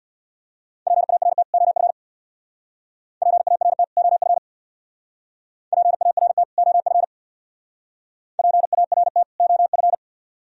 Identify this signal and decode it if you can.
Morse code, heard as a keyed tone, about 63 wpm, 700 Hz. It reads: PART OF PART OF PART OF PART OF